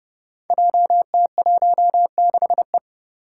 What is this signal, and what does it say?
Morse code, heard as a keyed tone, about 29 wpm, 700 Hz. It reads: JT16E
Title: Five Challenging Letter-Number Combinations 30wpm